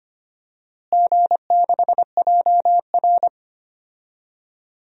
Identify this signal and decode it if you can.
Morse code, heard as a keyed tone, about 25 wpm, 700 Hz. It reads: G6JR
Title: DXCC-100 Call Signs - 4 Characters - Repeated 3x - 25wpm